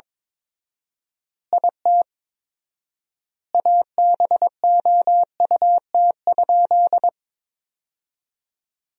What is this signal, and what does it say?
Morse code, heard as a keyed tone, about 22 wpm, 700 Hz. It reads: IT ABOUT?
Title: 4x Word-Spacing - Sentences from Top 100 Words - 22wpm